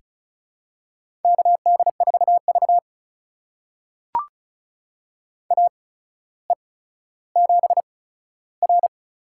Morse code, heard as a keyed tone, about 36 wpm, spaced as for 15 wpm, 700 Hz. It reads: KD4V AE7R
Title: USA Call Signs - 4 Characters - 8x Character Spacing - 35wpm